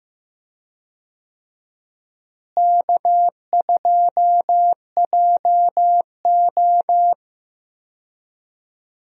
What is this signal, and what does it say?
Morse code, heard as a keyed tone, about 15 wpm, 700 Hz. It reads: K2JO
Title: USA Call Signs - 4 Characters - Repeated 2x - 15wpm